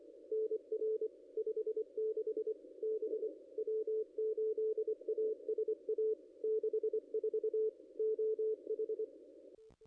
Morse code, heard as a keyed tone, about 24 wpm, 440 Hz. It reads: NR 56 B W8ASA 64 OH
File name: contests / ss-cw-k5tr-op-n5rz -0200